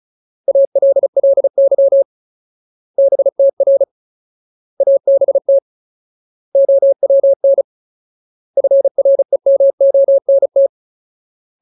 Morse code, heard as a keyed tone, about 34 wpm, 550 Hz. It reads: ALLY BTR ABT OWN FREMONT
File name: Ultimate Word List / 35WPM Ultimate